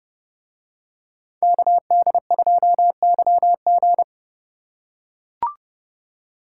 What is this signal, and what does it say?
Morse code, heard as a keyed tone, about 28 wpm, 700 Hz. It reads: KD2YG